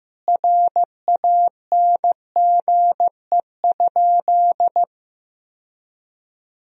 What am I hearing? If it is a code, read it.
Morse code, heard as a keyed tone, about 15 wpm, 700 Hz. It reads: RANGE?
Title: Sentences from Top 1000 Words - Review Part 4 - 15wpm